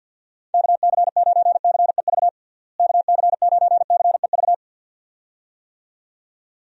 Morse code, heard as a keyed tone, about 50 wpm, 700 Hz. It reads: KC9CEV KC9CEV
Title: USA Call Signs - 6 Characters - Repeated 2x - 50wpm